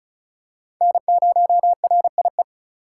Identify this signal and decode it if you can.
Morse code, heard as a keyed tone, about 34 wpm, 700 Hz. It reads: N0RIE
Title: USA Call Signs - 5 Characters - 3x Character Spacing - 35wpm